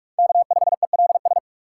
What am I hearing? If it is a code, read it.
Morse code, heard as a keyed tone, about 42 wpm, 700 Hz. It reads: K5ELS
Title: USA Call Signs - 5 Characters - 6x Character Spacing - 45wpm